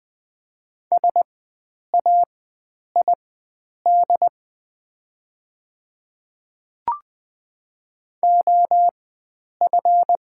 Morse code, heard as a keyed tone, about 20 wpm, spaced as for 9 wpm, 690 Hz. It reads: SAID OF
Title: 4x Character Spacing - Top 100 Words - 20wpm